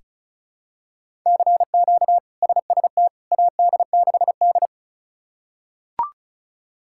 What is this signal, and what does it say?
Morse code, heard as a keyed tone, about 36 wpm, 700 Hz. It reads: CQ SST AD6D
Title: K1USN SST QSO Practice - 35wpm